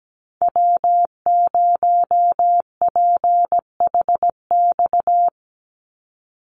Morse code, heard as a keyed tone, about 17 wpm, 700 Hz. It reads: W0PHX
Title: Call Signs Repeated 3x - 17wpm